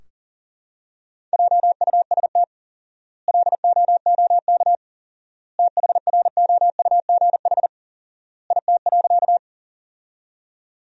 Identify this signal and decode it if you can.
Morse code, heard as a keyed tone, about 40 wpm, 700 Hz. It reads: JUST LOOK THROUGH IT.
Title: 4x Word-Spacing - Sentences from Top 200 Words - 40wpm